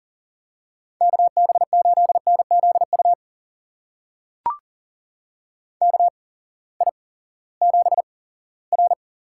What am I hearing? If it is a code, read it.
Morse code, heard as a keyed tone, about 39 wpm, spaced as for 20 wpm, 700 Hz. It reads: KB8NZU KI7R
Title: USA Call Signs - 6 Characters - 8x Character Spacing - 40wpm